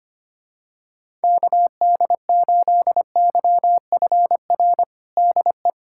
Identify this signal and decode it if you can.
Morse code, heard as a keyed tone, about 25 wpm, 700 Hz. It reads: KD8YFR DE
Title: USA Callsign DE Callsign 25wpm